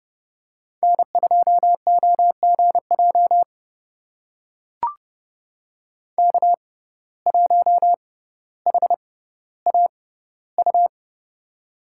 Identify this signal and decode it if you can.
Morse code, heard as a keyed tone, about 30 wpm, spaced as for 15 wpm, 690 Hz. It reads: N2OGJ K1HAU
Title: USA Call Signs - 5 Characters - 6x Character Spacing - 30wpm